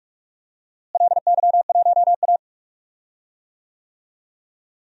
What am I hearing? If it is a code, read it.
Morse code, heard as a keyed tone, about 42 wpm, 700 Hz. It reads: LY1A
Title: DXCC-100 Call Signs - 4 Characters - Repeated 2x - 45wpm